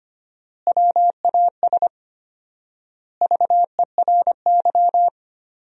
Morse code, heard as a keyed tone, about 25 wpm, 700 Hz. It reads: WAS VERY
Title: 4x Word-Spacing - Sentences from Top 500 Words - 25wpm